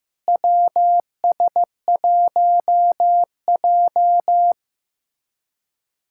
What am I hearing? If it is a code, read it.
Morse code, heard as a keyed tone, about 15 wpm, 700 Hz. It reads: WS1J
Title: USA Callsign DE Callsign - 4 Characters - 15wpm